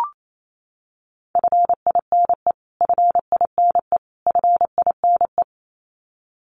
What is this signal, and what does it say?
Morse code, heard as a keyed tone, about 28 wpm, 700 Hz. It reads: FINE FINE FINE
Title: Top 500 Words Repeated 28wpm